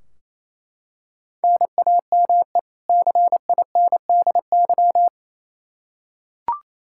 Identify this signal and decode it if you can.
Morse code, heard as a keyed tone, about 28 wpm, 700 Hz. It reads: NAME CINDY
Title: K1USN SST QSO Practice 28wpm